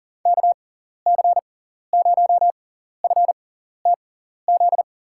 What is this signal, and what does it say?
Morse code, heard as a keyed tone, about 40 wpm, spaced as for 14 wpm, 700 Hz. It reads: KC0FTZ
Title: USA Call Signs - 6x Character Spacing - 40wpm